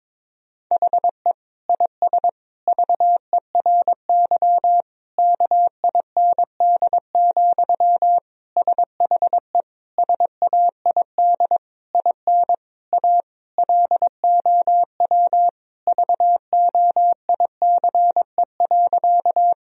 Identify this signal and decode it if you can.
Morse code, heard as a keyed tone, about 22 wpm, 700 Hz. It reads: HE IS VERY KIND, SHE SAID IN A LOW VOICE.
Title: Sentences from Top 1000 Words - Review Part 1 - 22wpm